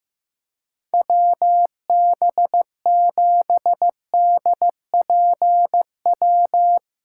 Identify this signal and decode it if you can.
Morse code, heard as a keyed tone, about 15 wpm, 700 Hz. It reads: WB7DPW
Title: USA Callsign DE Callsign - 6 Characters - 15wpm